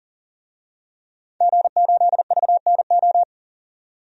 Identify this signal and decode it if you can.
Morse code, heard as a keyed tone, about 37 wpm, 700 Hz. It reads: G8VNO
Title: DXCC-100 Call Signs - 5 Characters - 40wpm